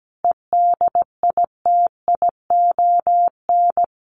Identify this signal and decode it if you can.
Morse code, heard as a keyed tone, about 17 wpm, 700 Hz. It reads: EDITION
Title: Sets of 5 Words Encore2 17wpm